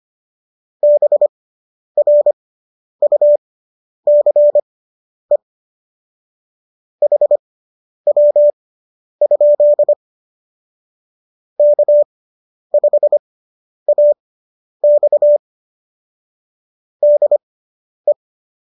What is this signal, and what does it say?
Morse code, heard as a keyed tone, about 25 wpm, spaced as for 10 wpm, 590 Hz. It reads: BRUCE HW? K5AX DE